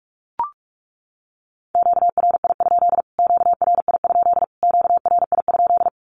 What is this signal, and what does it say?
Morse code, heard as a keyed tone, about 44 wpm, 700 Hz. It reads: QRI? QRI? QRI?
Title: Q-Codes Repeated 3x 45wpm